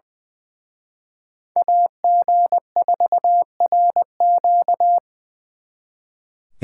Morse code, heard as a keyed tone, about 25 wpm, 700 Hz. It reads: AG4RQ
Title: Speed Racing - USA 2x2 Call Signs 20wpm